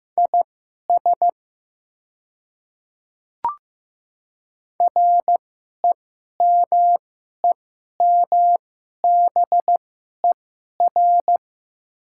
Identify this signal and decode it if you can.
Morse code, heard as a keyed tone, about 15 wpm, spaced as for 11 wpm, 700 Hz. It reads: IS REMEMBER
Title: 2x Character Spacing - Top 500 Words - 15wpm